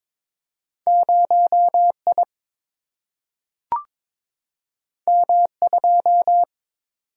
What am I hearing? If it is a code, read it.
Morse code, heard as a keyed tone, about 22 wpm, 700 Hz. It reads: 0I M2
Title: Single-Letter Single-Number 22wpm